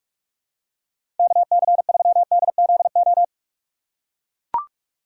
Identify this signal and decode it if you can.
Morse code, heard as a keyed tone, about 41 wpm, 700 Hz. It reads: KC3DZO
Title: USA Call Signs - 6 Characters - 6x Character Spacing - 45wpm